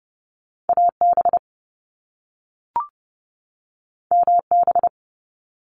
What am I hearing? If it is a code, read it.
Morse code, heard as a keyed tone, about 30 wpm, 700 Hz. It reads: AB MB
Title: Canadian Provinces and Territories 30wpm